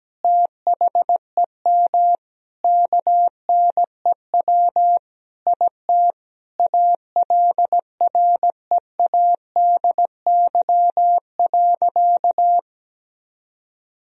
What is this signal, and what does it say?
Morse code, heard as a keyed tone, about 17 wpm, 700 Hz. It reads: THEM KNEW IT ALREADY.
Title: Sentences from Top 1000 Words - Review Part 3 - 17wpm